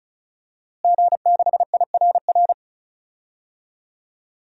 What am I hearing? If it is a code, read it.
Morse code, heard as a keyed tone, about 36 wpm, 700 Hz. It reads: G6IRR
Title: DXCC-100 Call Signs - 5 Characters - Repeated 3x - 35wpm